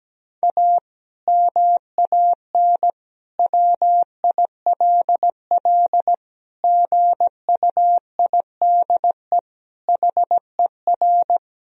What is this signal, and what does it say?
Morse code, heard as a keyed tone, about 17 wpm, 700 Hz. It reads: A MAN WILL GUIDE HER
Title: Sentences from Top 1000 Words - Review Part 3 - 17wpm